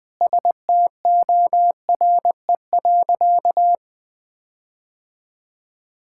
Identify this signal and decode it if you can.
Morse code, heard as a keyed tone, about 20 wpm, 700 Hz. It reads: STORE.
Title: Sentences with 9-Word Phrases - 20wpm